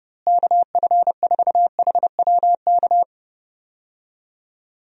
Morse code, heard as a keyed tone, about 30 wpm, 700 Hz. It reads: KF4HWK
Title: USA Call Signs - 6 Characters - Repeated 2x - 30wpm